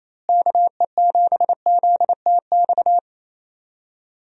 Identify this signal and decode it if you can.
Morse code, heard as a keyed tone, about 28 wpm, 700 Hz. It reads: KE7ZTX
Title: USA Call Signs - 6 Characters - Repeated 3x - 28wpm